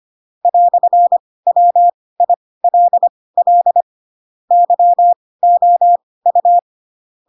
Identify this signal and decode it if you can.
Morse code, heard as a keyed tone, about 25 wpm, spaced as for 18 wpm, 700 Hz. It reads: "WILL YOU
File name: Pinocchio / Pinocchio ch6 12 25 ik1hge